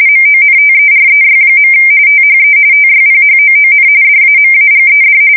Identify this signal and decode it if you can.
RTTY, about 45.45 baud. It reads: MONTEREY. THE TYPHOON DAMAGED
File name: fsk170 / USS-New-Jersey